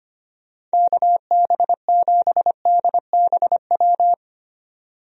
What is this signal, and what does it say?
Morse code, heard as a keyed tone, about 25 wpm, 700 Hz. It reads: KB7DBW
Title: USA Call Signs - 6 Characters - 25wpm